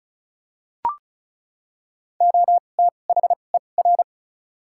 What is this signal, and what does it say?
Morse code, heard as a keyed tone, about 36 wpm, spaced as for 25 wpm, 700 Hz. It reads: OTHER
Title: 2x Character Spacing - Top 100 Words - 35wpm